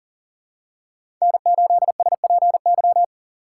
Morse code, heard as a keyed tone, about 40 wpm, 700 Hz. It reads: N8SPY
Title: USA Call Signs - 5 Characters - 2x Character Spacing - 40wpm